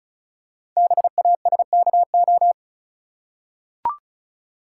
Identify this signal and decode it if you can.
Morse code, heard as a keyed tone, about 36 wpm, 690 Hz. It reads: BASKO
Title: Code Groups - 5 Letters - 35wpm